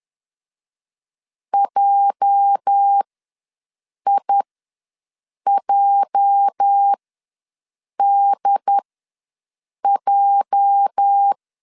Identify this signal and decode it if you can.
Morse code, heard as a keyed tone, about 10 wpm, spaced as for 6 wpm, 800 Hz. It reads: JIJDJ
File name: LEZIONI / 8 YJ45 / 8 YJ ws 10wpm 4043